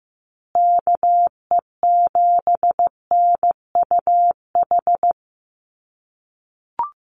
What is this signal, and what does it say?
Morse code, heard as a keyed tone, about 15 wpm, 700 Hz. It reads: KE7NUH
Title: USA Call Signs Repeated 3x 15wpm